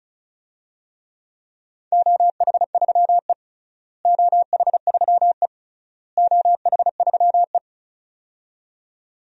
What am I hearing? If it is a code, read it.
Morse code, heard as a keyed tone, about 36 wpm, 690 Hz. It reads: OH3E OH3E OH3E
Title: DXCC-100 Call Signs - 4 Characters - Repeated 3x - 35wpm